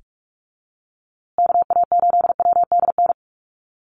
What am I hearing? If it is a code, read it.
Morse code, heard as a keyed tone, about 44 wpm, 700 Hz. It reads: KA8WDN
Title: USA Call Signs Repeated 3x 45wpm